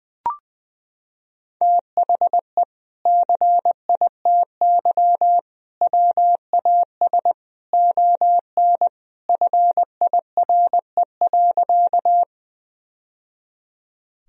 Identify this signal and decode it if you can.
Morse code, heard as a keyed tone, about 20 wpm, 700 Hz. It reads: THE CITY WAS ON FIRE.
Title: Sentences from Top 1000 Words - Review Part 1 - 20wpm